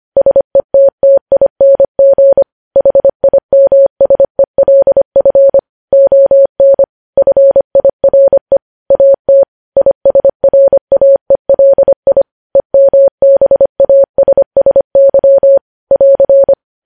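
Morse code, heard as a keyed tone, about 25 wpm, 550 Hz. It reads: SETTING HIMSELF ON FIRE AT ISRAELI EMBASSY <AR>
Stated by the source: Morse code news headlines at 25, opening with vvv.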